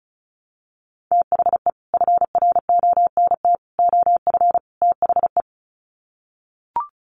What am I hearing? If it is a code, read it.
Morse code, heard as a keyed tone, about 34 wpm, 700 Hz. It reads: THE FRONT OF THE
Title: Sets of 4 Words 35wpm